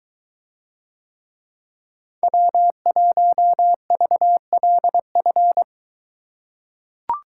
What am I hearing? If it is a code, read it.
Morse code, heard as a keyed tone, about 21 wpm, 700 Hz. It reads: W1VLF